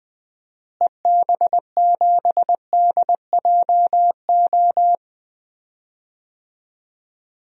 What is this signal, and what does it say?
Morse code, heard as a keyed tone, about 20 wpm, 700 Hz. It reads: EB7DJO
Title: DXCC-100 Call Signs - 6 Characters - Repeated 2x - 20wpm